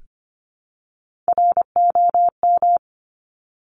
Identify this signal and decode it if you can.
Morse code, heard as a keyed tone, about 25 wpm, 700 Hz. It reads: ROM
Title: Top Three Letter Combinations in Words 25wpm